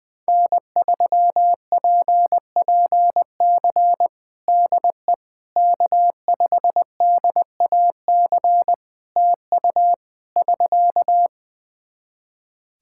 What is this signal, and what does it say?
Morse code, heard as a keyed tone, about 20 wpm, 700 Hz. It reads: N3PPC DE K5DAC TU <SK>
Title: K1USN SST QSO Practice 20wpm